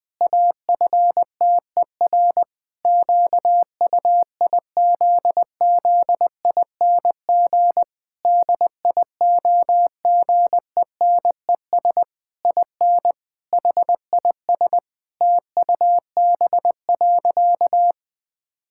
Morse code, heard as a keyed tone, about 20 wpm, 700 Hz. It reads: AFTER QUIZZING DIOGENES IN HIS TUB.
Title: Pangram Challenge v2 - 20wpm